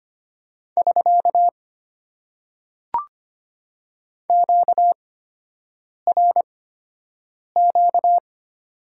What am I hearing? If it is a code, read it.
Morse code, heard as a keyed tone, about 25 wpm, spaced as for 7 wpm, 700 Hz. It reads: <SK> QRQ
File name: Top 90 QSO Elements - 8x Character Spacing - 25wpm